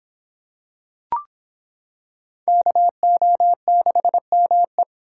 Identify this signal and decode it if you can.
Morse code, heard as a keyed tone, about 26 wpm, 700 Hz. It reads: KO6ME